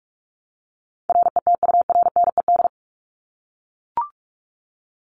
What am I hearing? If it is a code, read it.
Morse code, heard as a keyed tone, about 44 wpm, 700 Hz. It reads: RETURNED
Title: ED Suffix - Top 100 Words with Base Words 45wpm